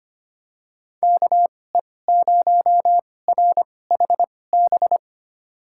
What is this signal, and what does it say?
Morse code, heard as a keyed tone, about 25 wpm, spaced as for 18 wpm, 700 Hz. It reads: KE0RHB
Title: USA Call Signs - 6 Characters - 2x Character Spacing - 25wpm